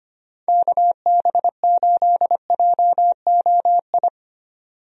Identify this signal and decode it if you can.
Morse code, heard as a keyed tone, about 25 wpm, 700 Hz. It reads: KB8JOI
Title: USA Call Signs - 6 Characters - 25wpm